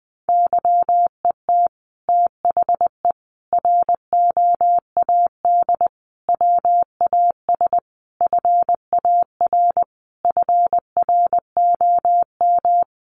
Morse code, heard as a keyed tone, about 20 wpm, 700 Hz. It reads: YET THE ROAD WAS FAR FROM
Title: Sentences from Top 1400 Words 20wpm